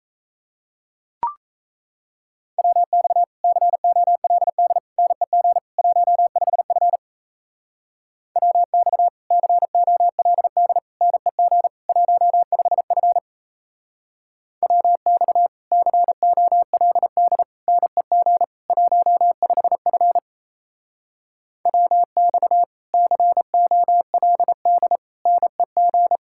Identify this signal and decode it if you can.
Morse code, heard as a keyed tone, about 34 wpm, 700 Hz. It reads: WX COLD NEG 15F WX COLD NEG 15F WX COLD NEG 15F WX COLD NEG
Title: Speed-Racing Common QSO Phrases Encore - 28wpm